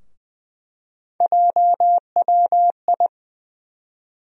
Morse code, heard as a keyed tone, about 20 wpm, 700 Hz. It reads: JWI
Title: USA Call Sign Postfixes - 3 Letters 20wpm